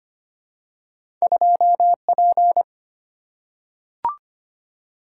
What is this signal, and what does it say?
Morse code, heard as a keyed tone, about 25 wpm, 700 Hz. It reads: 2P
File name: Two Letter-Number Combinations - 25wpm